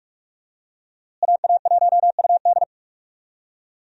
Morse code, heard as a keyed tone, about 46 wpm, 700 Hz. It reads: AA1UD
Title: USA Call Signs - 5 Characters - Repeated 2x - 45wpm